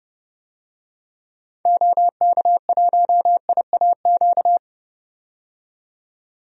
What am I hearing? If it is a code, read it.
Morse code, heard as a keyed tone, about 32 wpm, 700 Hz. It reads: OK1IAQ
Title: DXCC-100 Call Signs - 6 Characters - 30wpm